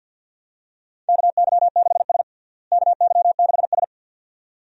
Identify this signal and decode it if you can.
Morse code, heard as a keyed tone, about 47 wpm, 700 Hz. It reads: KY6S KY6S
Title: USA Call Signs - 4 Characters - Repeated 2x - 50wpm